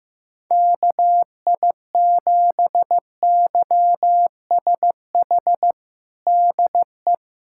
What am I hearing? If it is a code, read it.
Morse code, heard as a keyed tone, about 16 wpm, 700 Hz. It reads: KI7YSH DE
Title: USA Callsign DE Callsign 15wpm